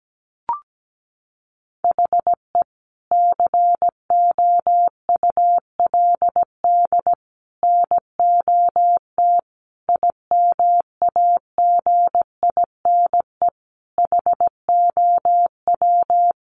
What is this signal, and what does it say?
Morse code, heard as a keyed tone, about 17 wpm, 700 Hz. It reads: HE COULD NOT IMAGINE HOW
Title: Sentences from Top 1400 Words 17wpm